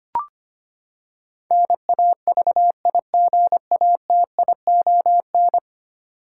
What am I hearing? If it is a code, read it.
Morse code, heard as a keyed tone, about 25 wpm, 700 Hz. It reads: NAVIGATION
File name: Ten Letter Words - 25wpm